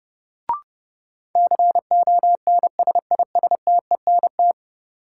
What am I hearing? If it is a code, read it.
Morse code, heard as a keyed tone, about 32 wpm, 700 Hz. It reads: CONSISTENT
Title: Sentences from Top 1500 Words - New Word Drill - 30wpm